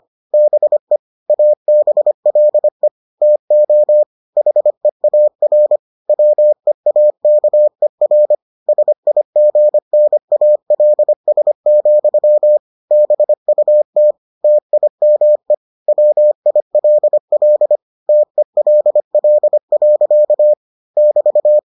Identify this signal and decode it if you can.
Morse code, heard as a keyed tone, about 25 wpm, 600 Hz. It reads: BE ABLE TO HEAR WEAKER SIGNALS, BUT TIME WILL TELL. <BT>